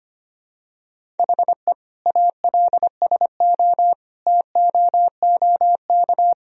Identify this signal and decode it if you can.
Morse code, heard as a keyed tone, about 25 wpm, 700 Hz. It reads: HE ALSO TOOK
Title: Sentences from Top 1000 Words - Review Part 1 - 25wpm